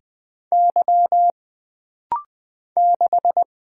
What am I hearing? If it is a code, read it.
Morse code, heard as a keyed tone, about 20 wpm, 700 Hz. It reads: Y 6
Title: Single Letter-Number Alt - Rapid-Fire 20wpm